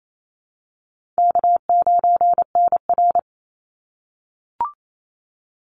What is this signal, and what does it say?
Morse code, heard as a keyed tone, about 28 wpm, 700 Hz. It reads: K9NR
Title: Easy Call Signs - 28wpm